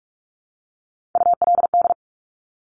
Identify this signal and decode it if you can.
Morse code, heard as a keyed tone, about 46 wpm, 700 Hz. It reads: ULD
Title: Top Three Letter Combinations in Words 45wpm